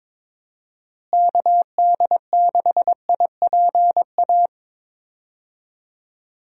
Morse code, heard as a keyed tone, about 22 wpm, 700 Hz. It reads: KD6IPA
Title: USA Call Signs - 6 Characters - Repeated 2x - 22wpm